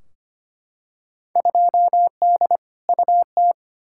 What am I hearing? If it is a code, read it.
Morse code, heard as a keyed tone, about 25 wpm, 700 Hz. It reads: 2D UT
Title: Field Day Exchanges Encore - 25wpm